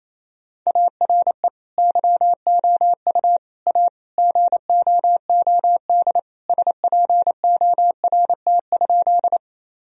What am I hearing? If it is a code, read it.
Morse code, heard as a keyed tone, about 28 wpm, 700 Hz. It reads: ARE YOU A GOOD SPORT?
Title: Sentences from Top 1000 Words - Review Part 4 - 28wpm